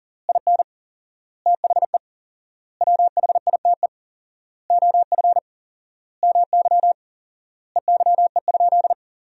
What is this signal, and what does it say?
Morse code, heard as a keyed tone, about 40 wpm, 700 Hz. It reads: IN THE WHITE OF MY EYE?
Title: 4x Word-Spacing - Sentences from Top 500 Words - 40wpm